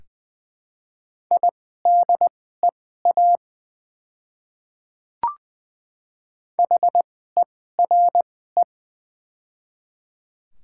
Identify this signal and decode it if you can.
Morse code, heard as a keyed tone, about 20 wpm, spaced as for 14 wpm, 700 Hz. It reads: IDEA HERE
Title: 2x Character Spacing - Top 300 Words - 20wpm